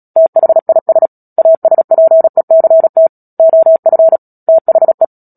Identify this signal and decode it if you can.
Morse code, heard as a keyed tone, about 37 wpm, 650 Hz. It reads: THIS ASPECT OF THE
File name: Book Two / 10-Chapter 10 - Epilogue - 36 WPM